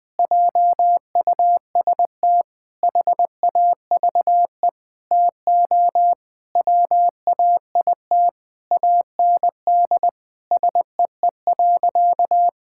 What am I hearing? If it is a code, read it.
Morse code, heard as a keyed tone, about 20 wpm, 700 Hz. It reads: JUST HAVE TO WAIT AND SEE.
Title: Sentences with 7-Word Phrases - Spoken Beginning - 20wpm